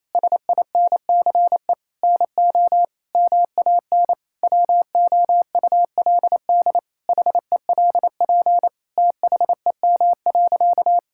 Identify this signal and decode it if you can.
Morse code, heard as a keyed tone, about 28 wpm, 700 Hz. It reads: SINCE NO MAN WOULD HELP THEM.
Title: Sentences from Top 1000 Words - Review Part 5 - 28wpm